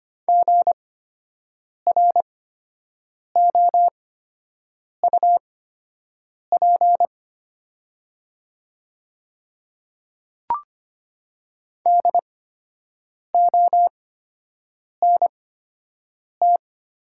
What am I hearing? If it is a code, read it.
Morse code, heard as a keyed tone, about 25 wpm, spaced as for 7 wpm, 700 Hz. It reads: GROUP DONT
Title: 8x Character Spacing - Top 300 Words - 25wpm